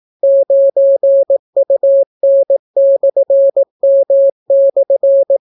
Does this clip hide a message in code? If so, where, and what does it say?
Yes: Morse code from 0.2 to 5.4 s, heard as a keyed tone, about 18 wpm, 550 Hz. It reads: 9UN/M/